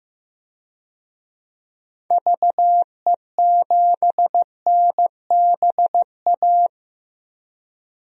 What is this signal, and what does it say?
Morse code, heard as a keyed tone, about 15 wpm, 700 Hz. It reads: VE7NBA
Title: DXCC-100 Call Signs - 6 Characters - Repeated 2x - 15wpm